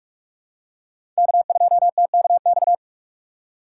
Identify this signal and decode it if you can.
Morse code, heard as a keyed tone, about 42 wpm, 700 Hz. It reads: K2TKX
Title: USA Call Signs - 5 Characters - 6x Character Spacing - 45wpm